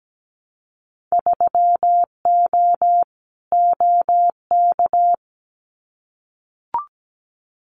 Morse code, heard as a keyed tone, about 17 wpm, 700 Hz. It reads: 3O OK
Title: Winter Field Day Exchanges Encore 17wpm